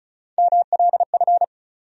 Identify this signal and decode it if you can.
Morse code, heard as a keyed tone, about 36 wpm, 700 Hz. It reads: MLF
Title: USA Call Sign Postfixes - 3 Letters 35wpm